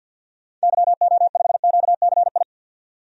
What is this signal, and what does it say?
Morse code, heard as a keyed tone, about 50 wpm, 700 Hz. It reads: YO5QCI
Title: DXCC-100 Call Signs - 6 Characters - 50wpm